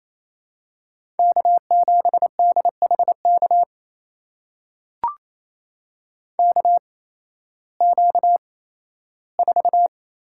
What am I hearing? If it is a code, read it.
Morse code, heard as a keyed tone, about 28 wpm, spaced as for 15 wpm, 700 Hz. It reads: K7DHK KQ4
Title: USA Call Signs - 5 Characters - 8x Character Spacing - 28wpm